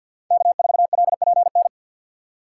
Morse code, heard as a keyed tone, about 51 wpm, 700 Hz. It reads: K4LPN
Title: USA Call Signs - 5 Characters - 4x Character Spacing - 50wpm